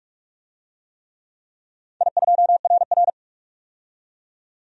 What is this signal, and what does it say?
Morse code, heard as a keyed tone, about 46 wpm, 700 Hz. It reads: I2RR
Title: DXCC-100 Call Signs - 4 Characters - 45wpm